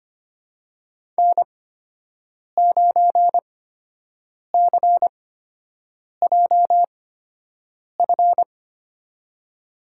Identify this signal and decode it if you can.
Morse code, heard as a keyed tone, about 25 wpm, spaced as for 7 wpm, 700 Hz. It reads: N9CJF
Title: USA Call Signs - 5 Characters - 8x Character Spacing - 25wpm